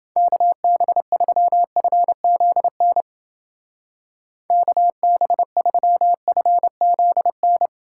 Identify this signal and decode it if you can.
Morse code, heard as a keyed tone, about 29 wpm, 700 Hz. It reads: KB3FZN KB3FZN